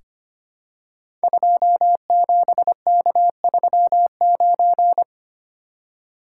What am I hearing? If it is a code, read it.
Morse code, heard as a keyed tone, about 25 wpm, 700 Hz. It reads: 27K39
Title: Code Groups - 5 Letter:Numbers - 25wpm